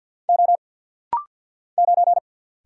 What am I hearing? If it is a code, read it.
Morse code, heard as a keyed tone, about 46 wpm, 700 Hz. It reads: O 9
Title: Single Letter-Number Alt - Rapid-Fire 50wpm